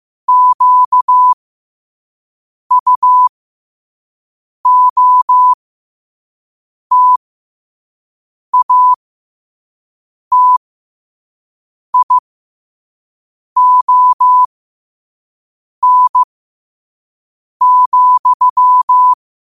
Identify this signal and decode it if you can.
Morse code, heard as a keyed tone, about 15 wpm, spaced as for 5 wpm, 1000 Hz. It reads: QUOTATION,
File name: qotdfiles / 5wpm